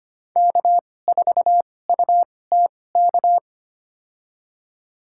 Morse code, heard as a keyed tone, about 25 wpm, spaced as for 18 wpm, 700 Hz. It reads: K4UTK
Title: USA Call Signs - 5 Characters - 2x Character Spacing - 25wpm